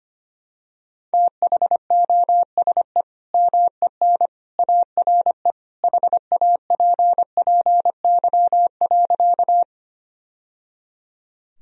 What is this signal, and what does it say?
Morse code, heard as a keyed tone, about 25 wpm, 700 Hz. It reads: THOSE MEN ARE HAPPY.
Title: Sentences from Top 1000 Words - Review Part 1 - 25wpm